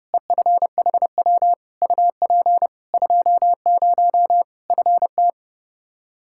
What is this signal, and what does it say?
Morse code, heard as a keyed tone, about 30 wpm, 700 Hz. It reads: EFHW UP 20 FT
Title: Practice QSO - Exchange 2 - Info - 30wpm